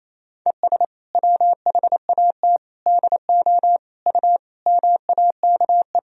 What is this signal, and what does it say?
Morse code, heard as a keyed tone, about 27 wpm, 700 Hz. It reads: ES WHAT DO U MAKE
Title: Sentences from Top 100 Words and Abbreviations 28wpm